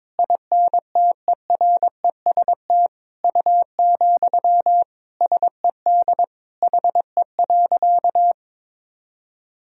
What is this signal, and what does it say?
Morse code, heard as a keyed tone, about 22 wpm, 700 Hz. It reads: INTEREST U, SED HE.
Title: Sentences from Top 400 Words and Abbreviations 22wpm